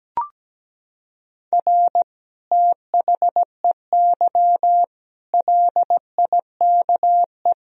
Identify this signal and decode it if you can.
Morse code, heard as a keyed tone, about 17 wpm, 700 Hz. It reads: R THEY LIKE
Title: Sentences from Top 200 Words and Abbreviations 17wpm